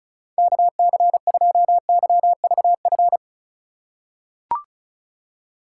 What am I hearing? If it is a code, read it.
Morse code, heard as a keyed tone, about 36 wpm, 700 Hz. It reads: KC2YVF
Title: USA Call Signs - 6 Characters - 2x Character Spacing - 35wpm